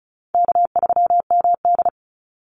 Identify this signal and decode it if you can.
Morse code, heard as a keyed tone, about 34 wpm, 700 Hz. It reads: K3MD
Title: Easy Call Signs Repeated 3x - 35wpm